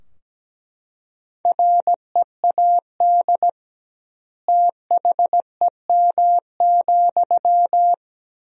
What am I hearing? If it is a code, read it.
Morse code, heard as a keyed tone, about 17 wpm, 700 Hz. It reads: READ THEM,
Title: 2x Word-Spacing - Sentences from Top 400 Words - 17wpm